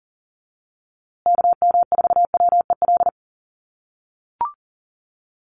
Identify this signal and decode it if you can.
Morse code, heard as a keyed tone, about 40 wpm, 700 Hz. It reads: KM4WEL
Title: USA Call Signs Repeated 2x 40wpm